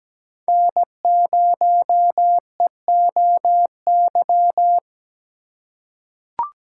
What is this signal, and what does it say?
Morse code, heard as a keyed tone, about 17 wpm, 700 Hz. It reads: N0EOY
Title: USA Call Signs - 5 Characters - 2x Character Spacing - 17wpm